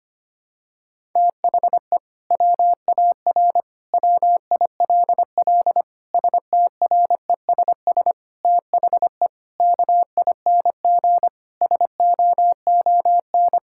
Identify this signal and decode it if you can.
Morse code, heard as a keyed tone, about 25 wpm, 700 Hz. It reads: THE WAR WILL STRESS THE KING SOON
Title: Sentences from Top 1000 Words - Review Part 6 - 25wpm